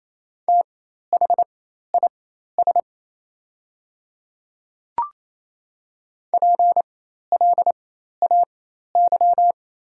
Morse code, heard as a keyed tone, about 27 wpm, spaced as for 13 wpm, 700 Hz. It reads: THIS PLAY
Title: 4x Character Spacing - Top 500 Words - 28wpm